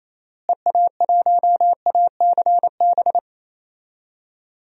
Morse code, heard as a keyed tone, about 28 wpm, 700 Hz. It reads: EA1ACB
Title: DXCC-100 Call Signs - 6 Characters - Repeated 3x - 28wpm